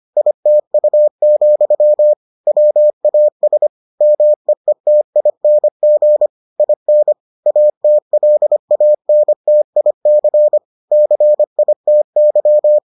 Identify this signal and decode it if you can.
Morse code, heard as a keyed tone, about 25 wpm, 600 Hz. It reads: ITU, WAS MEETING IN ATLANTIC CITY